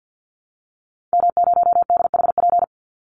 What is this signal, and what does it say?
Morse code, heard as a keyed tone, about 50 wpm, 700 Hz. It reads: M0DHP
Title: Call Signs Repeated 2x - 50wpm